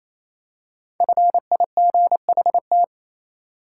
Morse code, heard as a keyed tone, about 28 wpm, 700 Hz. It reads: FIGHT
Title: Five Letter Words - 28wpm